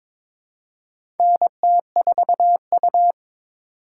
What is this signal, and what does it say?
Morse code, heard as a keyed tone, about 22 wpm, 700 Hz. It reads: NT4U
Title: USA Call Signs - 4 Characters - 2x Character Spacing - 22wpm